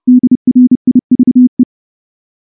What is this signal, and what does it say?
Morse code, heard as a keyed tone, about 30 wpm, 250 Hz. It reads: DRIVE
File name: morse-training-30WPM